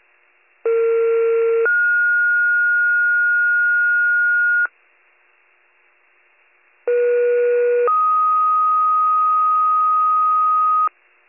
Two-tone page, heard as a one-second tone, then a three-second tone, about 460 then 1430 Hz, about 480 then 1250 Hz. It is heard from 0.6 to 10.9 s.